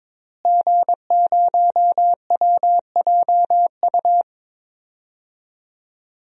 Morse code, heard as a keyed tone, about 22 wpm, 700 Hz. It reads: G0WJU
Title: DXCC-100 Call Signs - 5 Characters - Repeated 3x - 22wpm